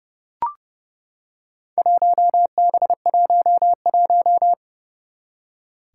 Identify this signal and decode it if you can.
Morse code, heard as a keyed tone, about 31 wpm, 700 Hz. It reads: 1B11
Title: Four Challenging Letter-Number Combinations 30wpm